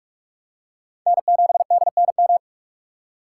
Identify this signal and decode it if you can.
Morse code, heard as a keyed tone, about 41 wpm, 700 Hz. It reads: N7DNM
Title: USA Call Signs - 5 Characters - 2x Character Spacing - 45wpm